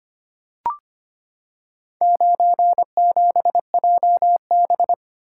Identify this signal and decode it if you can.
Morse code, heard as a keyed tone, about 25 wpm, 700 Hz. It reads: 97JB
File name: practice / 25wpm / Four Letter-Number Combinations - 25wpm